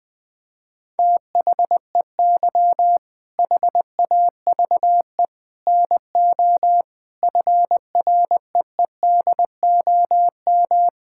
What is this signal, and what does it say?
Morse code, heard as a keyed tone, about 20 wpm, 700 Hz. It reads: THEY HAVE NO FREEDOM
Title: Sentences from Top 1000 Words - Review Part 5 - 20wpm